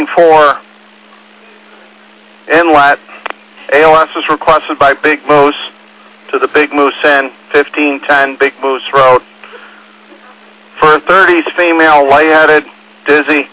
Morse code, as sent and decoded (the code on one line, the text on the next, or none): none